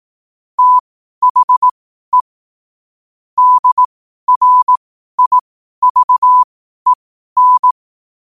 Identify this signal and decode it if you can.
Morse code, heard as a keyed tone, about 18 wpm, spaced as for 12 wpm, 1000 Hz. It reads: THE DRIVEN
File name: qotdfiles / 12wpm